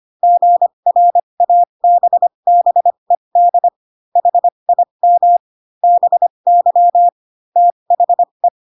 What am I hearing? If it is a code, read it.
Morse code, heard as a keyed tone, about 25 wpm, spaced as for 22 wpm, 700 Hz. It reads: GRABBED HIM BY THE
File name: Pinocchio / Pinocchio ch3 18 25 ik1hge